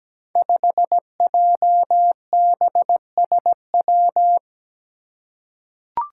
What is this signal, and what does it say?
Morse code, heard as a keyed tone, about 17 wpm, 700 Hz. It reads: 5JBSW
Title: Five Challenging Letter-Number Combinations 17wpm